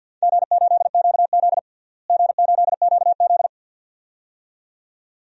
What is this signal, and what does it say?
Morse code, heard as a keyed tone, about 51 wpm, 700 Hz. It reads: G8QZ G8QZ
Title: DXCC-100 Call Signs - 4 Characters - Repeated 2x - 50wpm